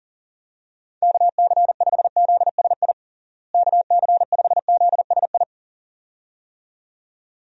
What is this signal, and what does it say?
Morse code, heard as a keyed tone, about 40 wpm, 700 Hz. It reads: KC5ZSI KC5ZSI
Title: USA Call Signs - 6 Characters - Repeated 2x - 40wpm